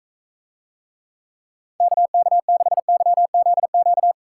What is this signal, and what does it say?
Morse code, heard as a keyed tone, about 42 wpm, 700 Hz. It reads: KK6YZQ